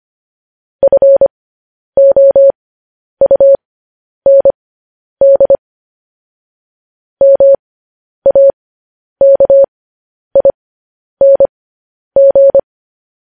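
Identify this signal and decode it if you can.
Morse code, heard as a keyed tone, about 25 wpm, spaced as for 10 wpm, 550 Hz. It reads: FOUND MAKING
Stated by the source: Morse code news headlines broadcast, opening with vvv.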